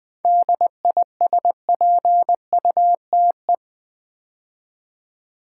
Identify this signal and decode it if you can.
Morse code, heard as a keyed tone, about 20 wpm, 700 Hz. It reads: DISPUTE
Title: 7 Letter Words - First 3 Letters Given - 20wpm